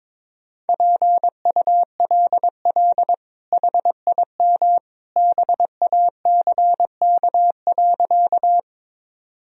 Morse code, heard as a keyed tone, about 22 wpm, 700 Hz. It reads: PULL HIM BACK.
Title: Sentences from Top 1000 Words - Review Part 2 - 22wpm